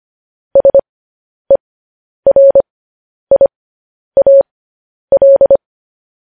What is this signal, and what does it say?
Morse code, heard as a keyed tone, about 25 wpm, spaced as for 10 wpm, 550 Hz. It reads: SERIAL
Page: news headlines in morse code at 10 wpm